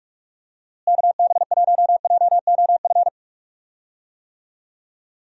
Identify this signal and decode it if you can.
Morse code, heard as a keyed tone, about 42 wpm, 700 Hz. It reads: KB1JOF
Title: USA Call Signs - 6 Characters - Repeated 2x - 45wpm